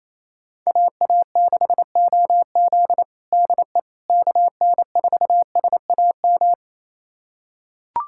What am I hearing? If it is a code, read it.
Morse code, heard as a keyed tone, about 28 wpm, 700 Hz. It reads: AA6OZ DE KN4SAM
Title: USA Callsign DE Callsign 28wpm